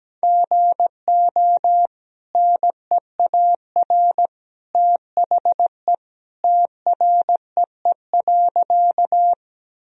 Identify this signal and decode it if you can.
Morse code, heard as a keyed tone, about 17 wpm, 700 Hz. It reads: GO NEAR THE TREE.
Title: Sentences from Top 1000 Words - Review Part 1 - 17wpm